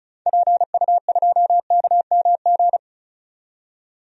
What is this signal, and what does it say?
Morse code, heard as a keyed tone, about 34 wpm, 700 Hz. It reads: PU2KMG
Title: DXCC-100 Call Signs - 6 Characters - 35wpm